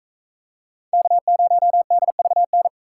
Morse code, heard as a keyed tone, about 44 wpm, 700 Hz. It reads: K0DVN